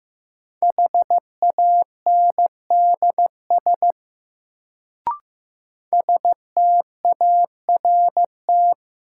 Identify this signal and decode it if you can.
Morse code, heard as a keyed tone, about 15 wpm, 700 Hz. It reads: HANDS START
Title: Sets of 4 Words - New Word Drill - 15wpm